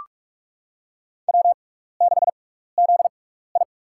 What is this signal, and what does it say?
Morse code, heard as a keyed tone, about 44 wpm, spaced as for 15 wpm, 700 Hz. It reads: W6ZI
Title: USA Call Signs - 4 Characters - 6x Character Spacing - 45wpm